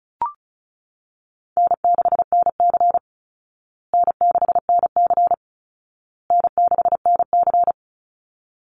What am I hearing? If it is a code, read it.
Morse code, heard as a keyed tone, about 34 wpm, 700 Hz. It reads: N6NC N6NC N6NC
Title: Easy Call Signs Repeated 3x - 35wpm